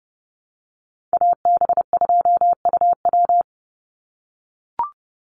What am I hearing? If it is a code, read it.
Morse code, heard as a keyed tone, about 30 wpm, 700 Hz. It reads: AB2UW
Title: USA Call Signs 30wpm